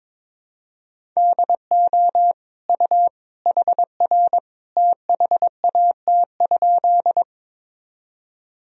Morse code, heard as a keyed tone, about 22 wpm, 700 Hz. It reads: DO U HR THAT?
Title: Sentences from Top 300 Words and Abbreviations 22wpm